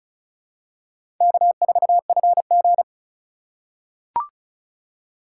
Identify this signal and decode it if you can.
Morse code, heard as a keyed tone, about 36 wpm, 700 Hz. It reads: K4FG
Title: USA Call Signs - 4x Character Spacing - 35wpm